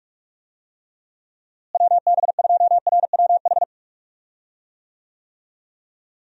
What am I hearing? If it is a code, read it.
Morse code, heard as a keyed tone, about 42 wpm, 700 Hz. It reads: WB2RWH
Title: USA Call Signs - 6 Characters - Repeated 3x - 45wpm